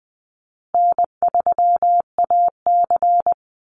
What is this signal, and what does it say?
Morse code, heard as a keyed tone, about 20 wpm, 700 Hz. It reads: N3AC
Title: Easy Call Signs Repeated 2x - 20wpm